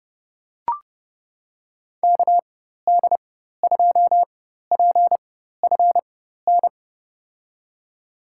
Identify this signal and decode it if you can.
Morse code, heard as a keyed tone, about 30 wpm, spaced as for 14 wpm, 700 Hz. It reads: KD2PFN
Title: USA Call Signs - 6 Characters - 4x Character Spacing - 30wpm